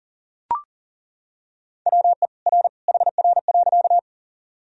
Morse code, heard as a keyed tone, about 40 wpm, 700 Hz. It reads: WE R HR.
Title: Sentences from Top 200 Words and Abbreviations 40wpm